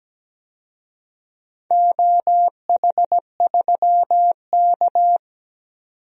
Morse code, heard as a keyed tone, about 17 wpm, 700 Hz. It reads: OH3K
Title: DXCC-100 Call Signs - 4 Characters - 17wpm